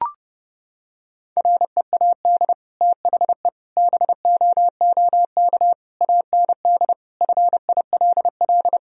Morse code, heard as a keyed tone, about 30 wpm, 700 Hz. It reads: READ THE BOOK AND FILL
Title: Sentences from Top 1000 Words - Review Part 3 - 30wpm